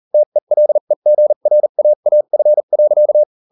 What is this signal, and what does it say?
Morse code, heard as a keyed tone, about 40 wpm, spaced as for 35 wpm, 600 Hz. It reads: TELEGRAAF.